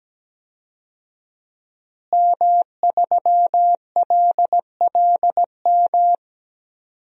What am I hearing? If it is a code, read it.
Morse code, heard as a keyed tone, about 17 wpm, 700 Hz. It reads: M3LLM
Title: DXCC-100 Call Signs - 5 Characters - Repeated 2x - 17wpm